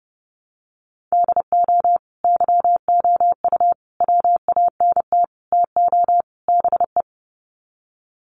Morse code, heard as a keyed tone, about 30 wpm, 700 Hz. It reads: DO YOU WANT TO BE
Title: Sets of 5 Words Encore2 30wpm